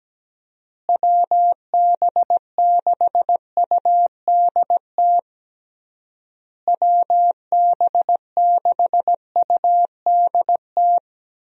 Morse code, heard as a keyed tone, about 17 wpm, 700 Hz. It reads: WB6UDT WB6UDT
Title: USA Call Signs - 6 Characters - Repeated 3x - 17wpm